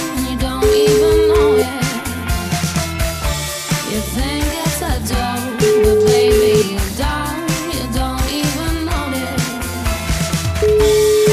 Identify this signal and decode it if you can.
Ringback tone (Europe, 425 Hz), with noise.